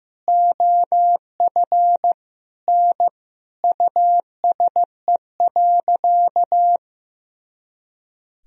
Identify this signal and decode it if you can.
Morse code, heard as a keyed tone, about 15 wpm, 700 Hz. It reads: OF N USE.
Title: Sentences from Top 100 Words and Abbreviations 15wpm